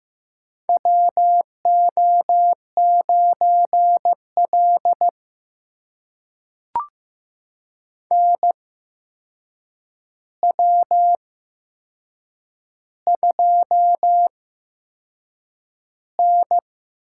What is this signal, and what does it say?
Morse code, heard as a keyed tone, about 15 wpm, spaced as for 6 wpm, 700 Hz. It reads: WO9L NW2N
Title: USA Call Signs - 4 Characters - 8x Character Spacing - 15wpm